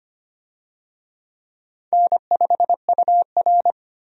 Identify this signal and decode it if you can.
Morse code, heard as a keyed tone, about 25 wpm, 700 Hz. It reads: N5UR
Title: USA Call Signs - 4 Characters - 25wpm